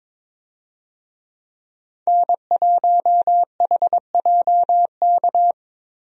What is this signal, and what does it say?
Morse code, heard as a keyed tone, about 21 wpm, 700 Hz. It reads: N1HJK
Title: USA Call Signs - 5 Characters - Repeated 2x - 22wpm